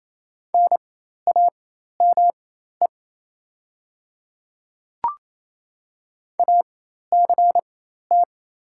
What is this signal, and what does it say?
Morse code, heard as a keyed tone, about 27 wpm, spaced as for 13 wpm, 690 Hz. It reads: NAME ACT
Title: 4x Character Spacing - Top 200 Words - 28wpm